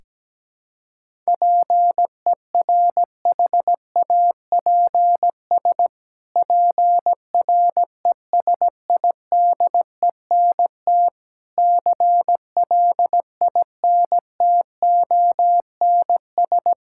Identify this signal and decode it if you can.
Morse code, heard as a keyed tone, about 17 wpm, 700 Hz. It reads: PERHAPS PRESIDENT CLINTONS
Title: Pangram Challenge v2 - 17wpm